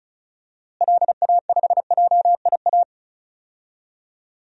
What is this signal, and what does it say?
Morse code, heard as a keyed tone, about 34 wpm, 700 Hz. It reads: LA5JIA
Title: DXCC-100 Call Signs - 6 Characters - Repeated 2x - 35wpm